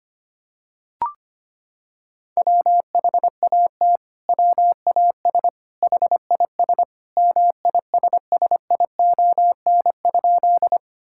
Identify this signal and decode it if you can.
Morse code, heard as a keyed tone, about 25 wpm, 700 Hz. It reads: WHAT WAS HIS MISSION?
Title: Sentences from Top 1600 Words 25wpm